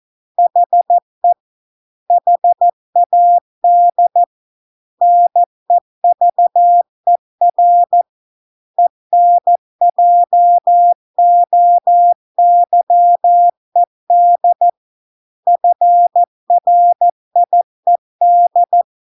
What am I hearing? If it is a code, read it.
Morse code, heard as a keyed tone, about 14 wpm, 700 Hz. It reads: HE HAD NEVER ENJOYED FRIED